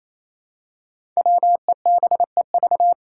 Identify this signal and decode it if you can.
Morse code, heard as a keyed tone, about 29 wpm, 700 Hz. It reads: WEBEV
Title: Five Challenging Letter-Number Combinations 28wpm